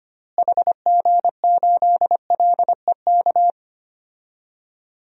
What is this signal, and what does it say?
Morse code, heard as a keyed tone, about 25 wpm, 700 Hz. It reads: HG8LEK
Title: DXCC-100 Call Signs - 6 Characters - Repeated 2x - 25wpm